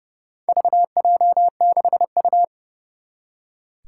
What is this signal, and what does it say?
Morse code, heard as a keyed tone, about 29 wpm, 700 Hz. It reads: VJ6U
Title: Four Challenging Letter-Number Combinations 30wpm